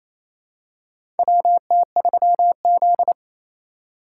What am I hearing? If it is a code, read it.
Morse code, heard as a keyed tone, about 28 wpm, 700 Hz. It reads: WT3Z
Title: USA Call Signs - 4 Characters - 3x Character Spacing - 28wpm